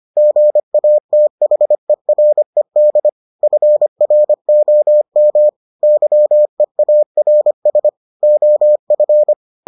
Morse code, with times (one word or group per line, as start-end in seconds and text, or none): 0.2-3.1 GATHERED
3.4-5.5 FROM
5.8-7.9 YEARS
8.2-9.3 OF